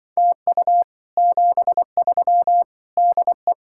Morse code, heard as a keyed tone, about 24 wpm, 700 Hz. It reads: TU 73 DE
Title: Speed-Racing Common QSO Phrases Encore - 20wpm